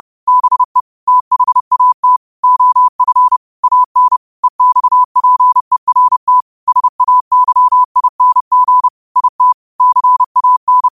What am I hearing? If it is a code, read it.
Morse code, heard as a keyed tone, about 30 wpm, 1000 Hz. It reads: BE THAT OF AN EXPERT SAYING IT CAN
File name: qotdfiles / 30wpm